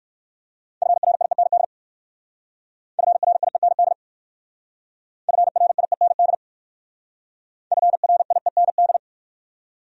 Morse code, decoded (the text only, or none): FRIEND FRIEND FRIEND FRIEND